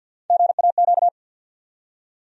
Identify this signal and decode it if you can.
Morse code, heard as a keyed tone, about 47 wpm, 700 Hz. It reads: GAQ
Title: USA Call Sign Postfixes - 3 Letters 50wpm